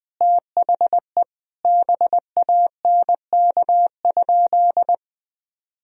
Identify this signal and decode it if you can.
Morse code, heard as a keyed tone, about 20 wpm, 700 Hz. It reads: THE BANK?
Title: Sentences from Top 1000 Words - Review Part 4 - 20wpm